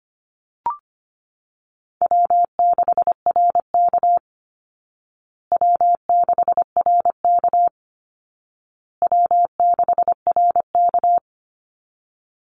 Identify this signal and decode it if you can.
Morse code, heard as a keyed tone, about 25 wpm, 700 Hz. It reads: W6RK W6RK W6RK
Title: Easy Call Signs Repeated 3x - 25wpm